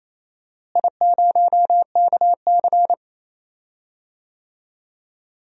Morse code, heard as a keyed tone, about 28 wpm, 700 Hz. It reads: I0KC
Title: DXCC-100 Call Signs - 4 Characters - Repeated 2x - 28wpm